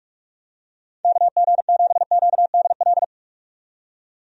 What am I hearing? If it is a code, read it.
Morse code, heard as a keyed tone, about 46 wpm, 700 Hz. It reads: KG7QDL
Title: USA Call Signs - 6 Characters - 45wpm